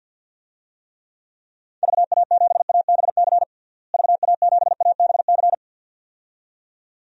Morse code, heard as a keyed tone, about 52 wpm, 700 Hz. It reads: VA7ABC VA7ABC
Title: DXCC-100 Call Signs - 6 Characters - Repeated 2x - 50wpm